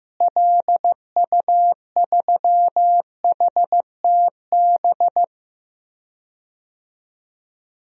Morse code, heard as a keyed tone, about 15 wpm, 700 Hz. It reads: LU3HTB
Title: DXCC-100 Call Signs - 6 Characters - Repeated 3x - 15wpm